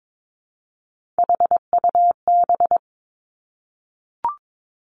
Morse code, heard as a keyed tone, about 22 wpm, 700 Hz. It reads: HUB
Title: Three Challenging Letter-Number Combinations 22wpm